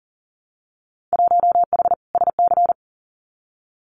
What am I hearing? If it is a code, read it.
Morse code, heard as a keyed tone, about 40 wpm, 700 Hz. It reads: 1H SC
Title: Winter Field Day Exchanges 40wpm